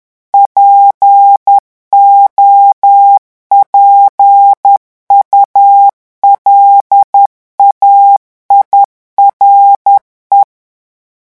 Morse code, heard as a keyed tone, about 10 wpm, 800 Hz. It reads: POPULAIRE
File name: morse mp3 / revision francais 10 mots minute